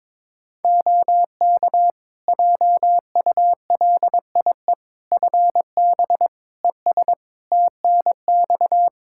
Morse code, heard as a keyed tone, about 22 wpm, 700 Hz. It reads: OK JULIE FB ES TNX
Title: Practice QSO - Exchange 2 - Info - 22wpm